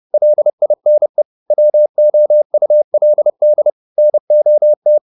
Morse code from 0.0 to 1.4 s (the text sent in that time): LINE